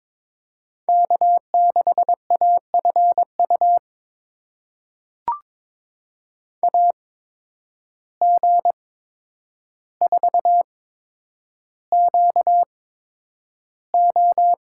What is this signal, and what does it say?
Morse code, heard as a keyed tone, about 22 wpm, spaced as for 9 wpm, 700 Hz. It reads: K6AFU AG4QO
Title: USA Call Signs - 5 Characters - 8x Character Spacing - 22wpm